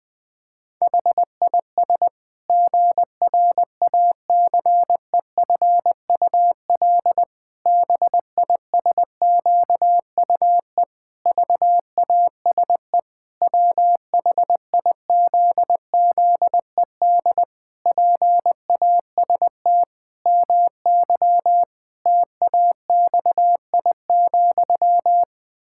Morse code, heard as a keyed tone, about 21 wpm, 700 Hz. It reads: HIS GRACEFUL BISQUE VASE WHIZZED PAST MY TAXI,
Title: Pangram Challenge v2 - 20wpm